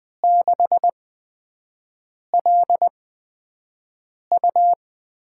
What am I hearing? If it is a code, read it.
Morse code, heard as a keyed tone, about 20 wpm, spaced as for 5 wpm, 700 Hz. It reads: 6LU
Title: USA Call Signs - 5 Characters - 8x Character Spacing - 20wpm